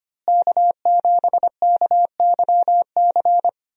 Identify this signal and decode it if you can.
Morse code, heard as a keyed tone, about 25 wpm, 700 Hz. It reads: K7KYC
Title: USA Call Signs - 5 Characters - Repeated 2x - 25wpm